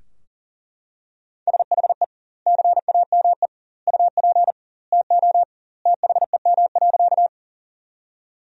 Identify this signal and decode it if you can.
Morse code, heard as a keyed tone, about 40 wpm, 700 Hz. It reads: SHE CAME UP TO THEM.
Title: 2x Word-Spacing - Sentences from Top 200 Words - 40wpm